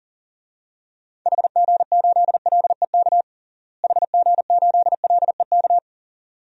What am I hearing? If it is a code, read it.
Morse code, heard as a keyed tone, about 40 wpm, 700 Hz. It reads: HG8LEK HG8LEK
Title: DXCC-100 Call Signs - 6 Characters - Repeated 2x - 40wpm